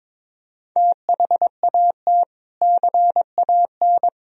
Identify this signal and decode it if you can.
Morse code, heard as a keyed tone, about 22 wpm, 700 Hz. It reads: THAT CAN
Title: Sentences from Top 1000 Words - Review Part 2 - 22wpm